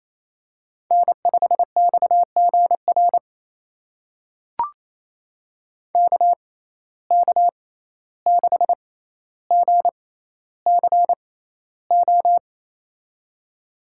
Morse code, heard as a keyed tone, about 28 wpm, spaced as for 14 wpm, 700 Hz. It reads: N5XGR KK6GCO
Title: USA Call Signs - 6x Character Spacing - 28wpm